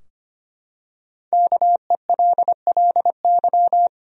Morse code, heard as a keyed tone, about 25 wpm, 700 Hz. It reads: KELLY
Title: USA Common Names - 25wpm